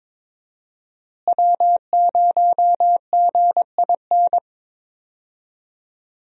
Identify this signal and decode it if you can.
Morse code, heard as a keyed tone, about 22 wpm, 700 Hz. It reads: W0GIN
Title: USA Call Signs - 5 Characters - Repeated 2x - 22wpm